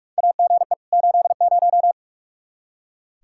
Morse code, heard as a keyed tone, about 46 wpm, 700 Hz. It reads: AGE 80
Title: K1USN SST QSO Practice 45wpm